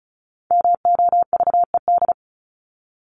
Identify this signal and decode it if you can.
Morse code, heard as a keyed tone, about 34 wpm, 700 Hz. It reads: MOVED
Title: ED Suffix - Top 100 Words with Base Words 35wpm